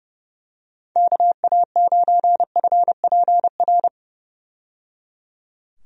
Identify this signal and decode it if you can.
Morse code, heard as a keyed tone, about 30 wpm, 700 Hz. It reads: KA9FPR
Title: USA Call Signs - 6 Characters - 30wpm